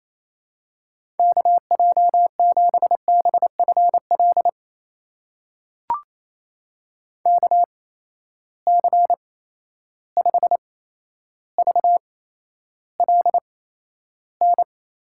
Morse code, heard as a keyed tone, about 28 wpm, spaced as for 12 wpm, 700 Hz. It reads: KJ7BFL KC5VLN
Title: USA Call Signs - 6 Characters - 8x Character Spacing - 28wpm